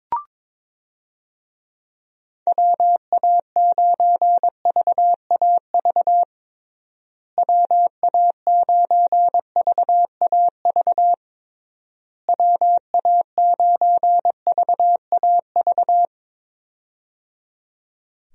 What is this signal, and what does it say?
Morse code, heard as a keyed tone, about 22 wpm, 700 Hz. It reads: WA9VAV WA9VAV WA9VAV
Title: USA Call Signs - 6 Characters - Repeated 3x - 22wpm